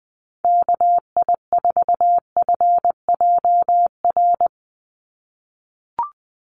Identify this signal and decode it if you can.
Morse code, heard as a keyed tone, about 20 wpm, 700 Hz. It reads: KI4FJR
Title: USA Call Signs 20wpm